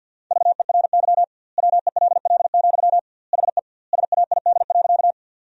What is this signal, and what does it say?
Morse code, heard as a keyed tone, about 50 wpm, 700 Hz. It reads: VERY WELL, HE SAID.
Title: 2x Word-Spacing - Sentences from Top 200 Words - 50wpm